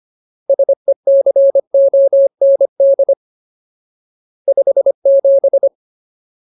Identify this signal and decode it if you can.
Morse code, heard as a keyed tone, about 25 wpm, 550 Hz. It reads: SECOND 57
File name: Ultimate Word List / 25WPM Ultimate